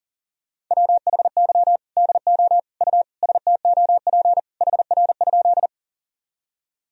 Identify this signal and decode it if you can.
Morse code, heard as a keyed tone, about 40 wpm, 700 Hz. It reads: WHY DO U STOP HR?
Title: Sentences from Top 300 Words and Abbreviations 40wpm